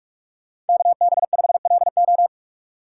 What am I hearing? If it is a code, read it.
Morse code, heard as a keyed tone, about 55 wpm, 700 Hz. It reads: KB5LO
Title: Speed Racing - USA Call Signs 45wpm